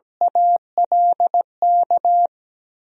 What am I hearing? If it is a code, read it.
Morse code, heard as a keyed tone, about 17 wpm, 700 Hz. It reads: ALK
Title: 3x Word-Spacing - Sentences from Top 400 Words - 17wpm